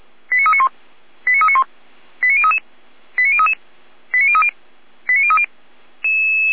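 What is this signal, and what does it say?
Five-tone selective call (ZVEI), heard as quick runs of five tones, tones of about 70 ms. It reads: 89391 89391 89033 89033 89030 89030
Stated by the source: ZVEI five-tone selective call received on an SDR.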